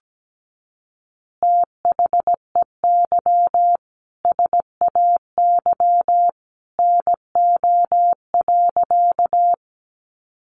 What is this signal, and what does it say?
Morse code, heard as a keyed tone, about 17 wpm, 700 Hz. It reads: THEY SAY NO.
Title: Easy Phrases 17wpm